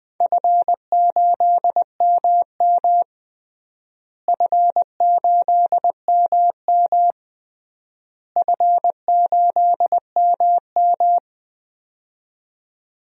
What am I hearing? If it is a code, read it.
Morse code, heard as a keyed tone, about 20 wpm, 700 Hz. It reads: F8MM F8MM F8MM
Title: DXCC-100 Call Signs - 4 Characters - Repeated 3x - 20wpm